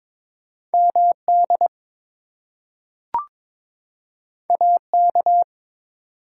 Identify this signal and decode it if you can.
Morse code, heard as a keyed tone, about 22 wpm, 700 Hz. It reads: MD AK
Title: US States and Canadian Provinces 22wpm